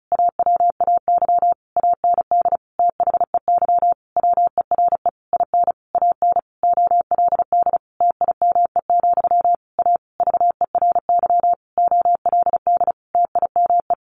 Morse code, heard as a keyed tone, about 34 wpm, 700 Hz. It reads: AWAY AND THEY WERE IN AN OLD TIME, A VERY OLD TIME
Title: A Sound Of Thunder 35wpm